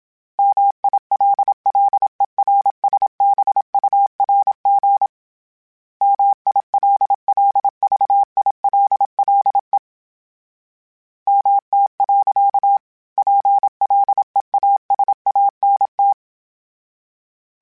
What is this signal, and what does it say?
Morse code, heard as a keyed tone, about 27 wpm, 800 Hz. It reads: MILLERSBURG MILLVILLE MT. PLEASANT
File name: fromSMs / FromSMs mp3s / Cities@25